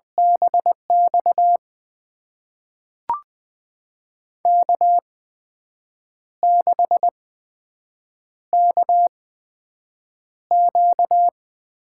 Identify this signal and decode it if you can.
Morse code, heard as a keyed tone, about 20 wpm, spaced as for 7 wpm, 700 Hz. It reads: BX K6KQ
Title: USA Call Signs - 8x Character Spacing - 20wpm